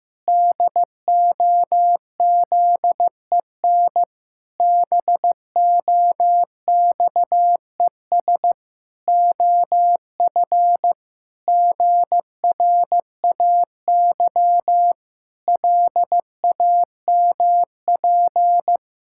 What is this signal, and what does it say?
Morse code, heard as a keyed tone, about 15 wpm, 700 Hz. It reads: DOZEN BOXES OF GRAY LAMP
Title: Pangram Challenge v2 - 15wpm